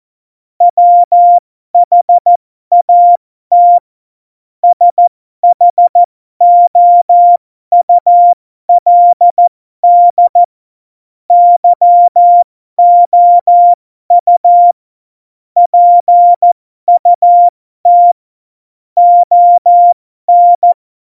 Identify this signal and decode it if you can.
Morse code, heard as a keyed tone, about 14 wpm, spaced as for 12 wpm, 700 Hz. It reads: WHAT SHOULD YOU PUT ON
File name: Morsekurs / Witze